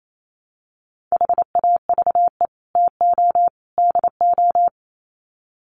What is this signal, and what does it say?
Morse code, heard as a keyed tone, about 28 wpm, 700 Hz. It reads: HAVE TO DO
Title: Sets of 3 Words 28wpm